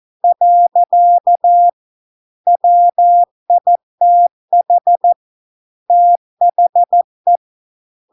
Morse code, heard as a keyed tone, about 14 wpm, 700 Hz. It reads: . WITH THE